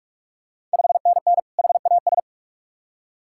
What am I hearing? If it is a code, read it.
Morse code, heard as a keyed tone, about 45 wpm, 690 Hz. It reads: 5NN HAS
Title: Ontario QSO Party Exchange 45wpm